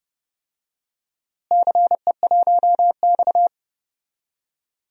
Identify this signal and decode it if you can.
Morse code, heard as a keyed tone, about 31 wpm, 700 Hz. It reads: CE1X
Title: DXCC-100 Call Signs - 4 Characters - Repeated 2x - 30wpm